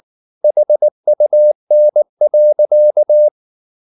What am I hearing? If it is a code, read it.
Morse code, heard as a keyed tone, about 19 wpm, 600 Hz. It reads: HUN.
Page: Ferie 19 wpm